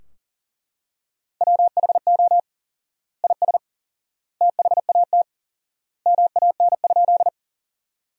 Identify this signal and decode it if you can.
Morse code, heard as a keyed tone, about 37 wpm, 700 Hz. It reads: WHO IS THAT MAN?
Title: 4x Word-Spacing - Sentences from Top 200 Words - 40wpm